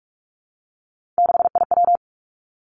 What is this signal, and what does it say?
Morse code, heard as a keyed tone, about 46 wpm, 700 Hz. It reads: 6IW
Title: Three Challenging Letter-Number Combinations 45wpm